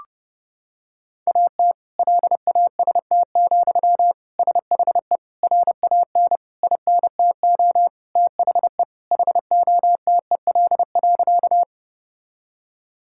Morse code, heard as a keyed tone, about 32 wpm, 700 Hz. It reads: AT LAST, SHE RAN INTO THE HOTEL.
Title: Sentences from Top 1600 Words 30wpm